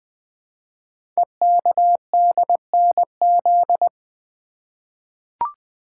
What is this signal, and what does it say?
Morse code, heard as a keyed tone, about 20 wpm, 700 Hz. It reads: EKDNZ
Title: Code Groups - 5 Letters - 20wpm